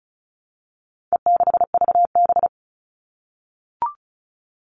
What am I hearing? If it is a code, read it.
Morse code, heard as a keyed tone, about 34 wpm, 700 Hz. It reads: E6VB
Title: All Cut Numbers - 4-Digit 35wpm